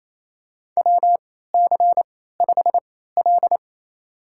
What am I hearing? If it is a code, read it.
Morse code, heard as a keyed tone, about 27 wpm, spaced as for 16 wpm, 700 Hz. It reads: WC5L
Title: USA Call Signs - 4 Characters - 3x Character Spacing - 28wpm